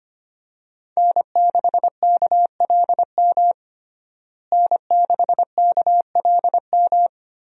Morse code, heard as a keyed tone, about 25 wpm, 700 Hz. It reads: N6KLM N6KLM
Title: USA Call Signs - 5 Characters - Repeated 2x - 25wpm